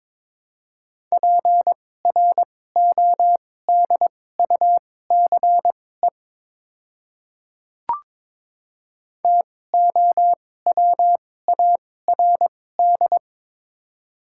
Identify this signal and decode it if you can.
Morse code, heard as a keyed tone, about 22 wpm, spaced as for 16 wpm, 700 Hz. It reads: PRODUCE TOWARD
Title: 2x Character Spacing - Top 500 Words - 22wpm